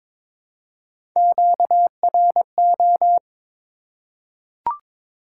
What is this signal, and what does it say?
Morse code, heard as a keyed tone, about 22 wpm, 700 Hz. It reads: QRO
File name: Top 90 QSO Elements - 6x Character Spacing - 22wpm